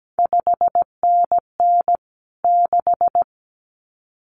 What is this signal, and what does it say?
Morse code, heard as a keyed tone, about 17 wpm, 700 Hz. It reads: 5NN 6
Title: CQ WW DX Contest Exchange - 17wpm